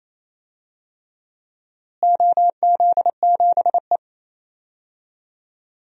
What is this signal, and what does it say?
Morse code, heard as a keyed tone, about 28 wpm, 700 Hz. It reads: OZ7E
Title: DXCC-100 Call Signs - 4 Characters - Repeated 3x - 28wpm